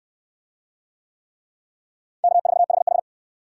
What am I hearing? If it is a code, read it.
Morse code, heard as a keyed tone, about 65 wpm, 700 Hz. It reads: K4HH